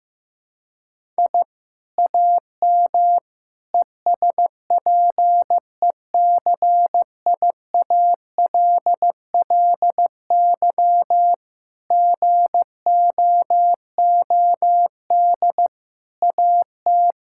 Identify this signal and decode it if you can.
Morse code, heard as a keyed tone, about 15 wpm, 700 Hz. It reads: I AM ESPECIALLY GOOD AT
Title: Sentences from Top 1000 Words - Review Part 6 - 15wpm